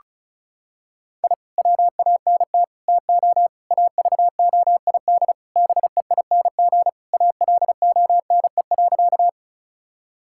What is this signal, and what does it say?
Morse code, heard as a keyed tone, about 36 wpm, 700 Hz. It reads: I WANT TO AVOID BEING ALONE.
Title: Sentences from Top 1000 Words - Review Part 6 - 35wpm